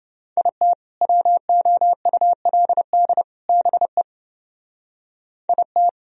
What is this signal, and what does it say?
Morse code, heard as a keyed tone, about 29 wpm, 700 Hz. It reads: IT WOULD BE IT